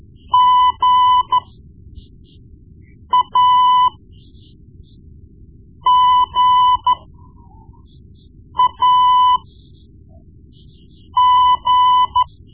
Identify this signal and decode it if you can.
Morse code, heard as a keyed tone, about 13 wpm, spaced as for 4 wpm, 1000 Hz. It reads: GAGAG
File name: Morse Lessons / Morse Lessons / Tape-1B